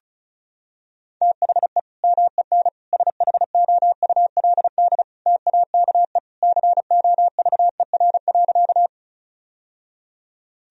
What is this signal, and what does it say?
Morse code, heard as a keyed tone, about 36 wpm, 700 Hz. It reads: THE MEN SHOULD TAKE COVER.
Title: Sentences from Top 1000 Words - Review Part 1 - 35wpm